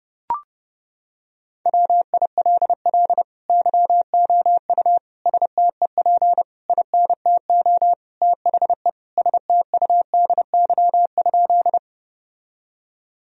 Morse code, heard as a keyed tone, about 30 wpm, 700 Hz. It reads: WILL YOU STEP INTO THE STUDY?
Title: Sentences from Top 1000 Words - Review Part 2 - 30wpm